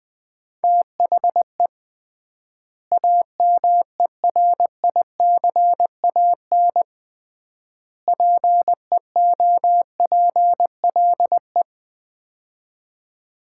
Morse code, heard as a keyed tone, about 20 wpm, 700 Hz. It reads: THE AMERICAN PEOPLE
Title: 3x Word-Spacing - Sets of 3 Words - 20wpm